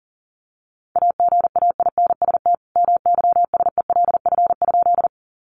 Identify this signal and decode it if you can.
Morse code, heard as a keyed tone, about 40 wpm, 700 Hz. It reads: AGAINST MYSELF?
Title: Sentences from Top 1300 Words 40wpm